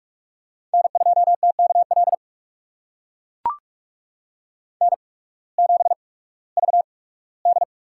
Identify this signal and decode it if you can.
Morse code, heard as a keyed tone, about 42 wpm, spaced as for 21 wpm, 700 Hz. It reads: N2TKL N7VD
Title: USA Call Signs - 8x Character Spacing - 45wpm